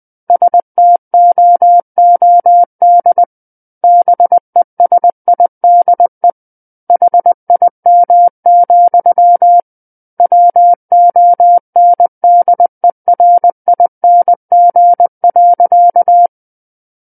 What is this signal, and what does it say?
Morse code, heard as a keyed tone, about 20 wpm, 700 Hz. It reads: STOOD BESIDE HIM, WONDERING.
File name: 20 WPM